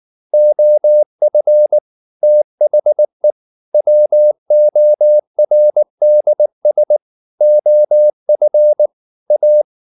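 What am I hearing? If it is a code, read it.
Morse code, heard as a keyed tone, about 19 wpm, 600 Hz. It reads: OF THE WORDS OF A
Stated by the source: Morse code at 19 wpm from CW med Gnister, reading Ken Follett's novel Never in English.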